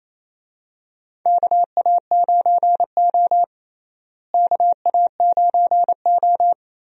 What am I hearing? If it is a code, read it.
Morse code, heard as a keyed tone, about 28 wpm, 700 Hz. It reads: KA9O KA9O
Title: USA Call Signs - 4 Characters - Repeated 2x - 28wpm